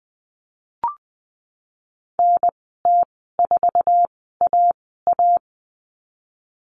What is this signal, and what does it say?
Morse code, heard as a keyed tone, about 20 wpm, spaced as for 15 wpm, 700 Hz. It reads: NT4AA
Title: 048 Lesson 9g - Number 4 - Call Signs 20wpm